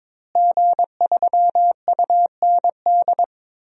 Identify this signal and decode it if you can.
Morse code, heard as a keyed tone, about 22 wpm, 700 Hz. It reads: G3UND
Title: DXCC-100 Call Signs - 5 Characters - Repeated 3x - 22wpm